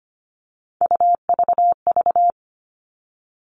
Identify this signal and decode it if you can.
Morse code, heard as a keyed tone, about 25 wpm, 700 Hz. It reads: UVV
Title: Three Challenging Letter-Number Combinations 25wpm